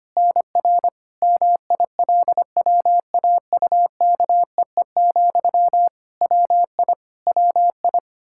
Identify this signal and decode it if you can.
Morse code, heard as a keyed tone, about 25 wpm, 700 Hz. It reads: NR MILWAUKEE, WI WI
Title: Practice QSO - Exchange 1 - Beginning - 25wpm